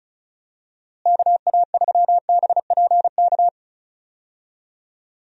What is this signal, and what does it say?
Morse code, heard as a keyed tone, about 36 wpm, 700 Hz. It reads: KA3BPK
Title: USA Call Signs - 6 Characters - Repeated 3x - 35wpm